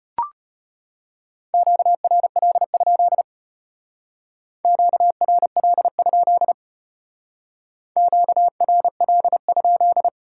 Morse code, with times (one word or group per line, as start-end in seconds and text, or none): 1.5-3.2 QRL?
4.7-6.5 QRL?
8.0-10.1 QRL?